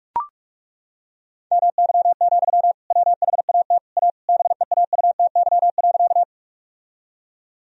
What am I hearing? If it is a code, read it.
Morse code, heard as a keyed tone, about 41 wpm, 700 Hz. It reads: MY, WHAT A BEAUTY.
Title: Sentences from Top 1000 Words - Review Part 2 - 45wpm